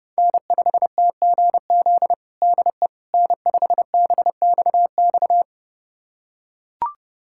Morse code, heard as a keyed tone, about 30 wpm, 700 Hz. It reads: N5TGZ DE N5BXX
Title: USA Callsign DE Callsign - 5 Characters - 30wpm